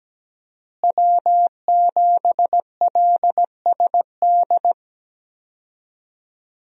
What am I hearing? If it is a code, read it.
Morse code, heard as a keyed tone, about 17 wpm, 700 Hz. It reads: W7LSD
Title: USA Call Signs - 5 Characters - Repeated 3x - 17wpm